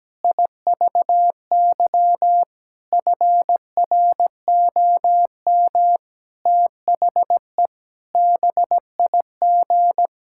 Morse code, heard as a keyed tone, about 17 wpm, 700 Hz. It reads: IVY FROM THE BIG
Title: Pangram Challenge v2 - 17wpm